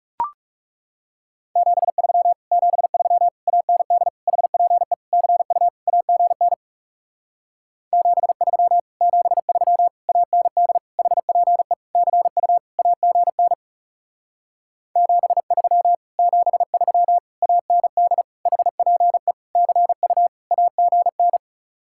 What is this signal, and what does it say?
Morse code, heard as a keyed tone, about 40 wpm, 700 Hz. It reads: 73 73 AND HPE CU AGN 73 73 AND HPE CU AGN 73 73 AND HPE CU AGN